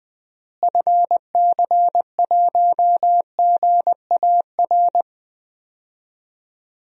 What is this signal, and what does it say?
Morse code, heard as a keyed tone, about 20 wpm, 700 Hz. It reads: FC1GAR
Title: DXCC-100 Call Signs - 6 Characters - Repeated 3x - 20wpm